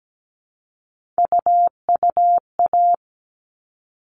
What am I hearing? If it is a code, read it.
Morse code, heard as a keyed tone, about 17 wpm, 700 Hz. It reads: UUA
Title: All Cut Numbers - 3-Digit 17wpm